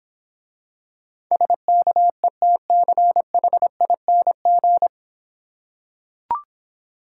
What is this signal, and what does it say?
Morse code, heard as a keyed tone, about 26 wpm, 700 Hz. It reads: SKETCHING